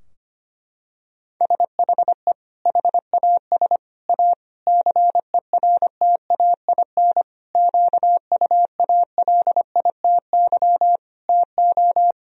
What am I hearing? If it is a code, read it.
Morse code, heard as a keyed tone, about 25 wpm, 700 Hz. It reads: SHE HAS A CERTAIN QUALITY TO
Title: Sentences from Top 1000 Words - Review Part 4 - 25wpm